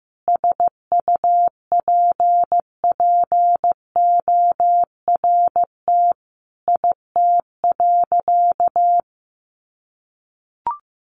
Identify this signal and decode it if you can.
Morse code, heard as a keyed tone, about 15 wpm, 700 Hz. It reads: SUPPORT IT.
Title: Sentences from Top 1300 Words 15wpm